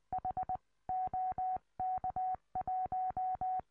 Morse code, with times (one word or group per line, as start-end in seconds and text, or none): 0.1-0.6 H
0.9-3.6 OK1